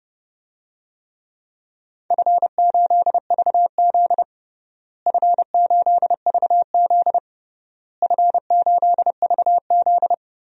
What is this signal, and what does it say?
Morse code, heard as a keyed tone, about 30 wpm, 700 Hz. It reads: F8VZ F8VZ F8VZ
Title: DXCC-100 Call Signs - 4 Characters - Repeated 3x - 30wpm